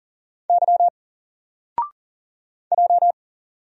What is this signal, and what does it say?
Morse code, heard as a keyed tone, about 40 wpm, 700 Hz. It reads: Y J
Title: Single Letter-Number Alt 40wpm